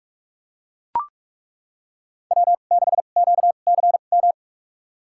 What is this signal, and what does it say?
Morse code, heard as a keyed tone, about 41 wpm, spaced as for 31 wpm, 700 Hz. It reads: W6QCM
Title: USA Call Signs - 5 Characters - 2x Character Spacing - 45wpm